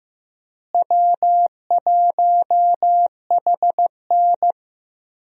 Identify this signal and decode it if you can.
Morse code, heard as a keyed tone, about 15 wpm, 700 Hz. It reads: W1HN
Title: USA Call Signs - 4 Characters - Repeated 3x - 15wpm